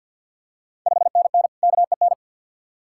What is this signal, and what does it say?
Morse code, heard as a keyed tone, about 46 wpm, 700 Hz. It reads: 5NN KEN
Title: Ontario QSO Party Exchange 50wpm